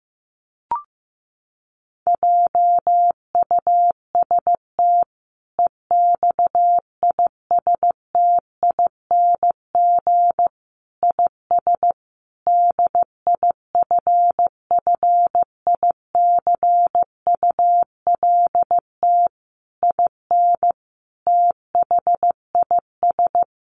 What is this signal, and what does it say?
Morse code, heard as a keyed tone, about 15 wpm, 700 Hz. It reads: JUST EXISTING IS DIFFICULT IN THIS
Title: Sentences from Top 1500 Words 15wpm